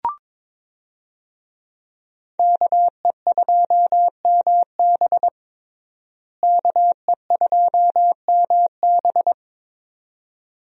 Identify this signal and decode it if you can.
Morse code, heard as a keyed tone, about 22 wpm, 700 Hz. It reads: KE2MB KE2MB
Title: USA Call Signs - 5 Characters - Repeated 2x - 22wpm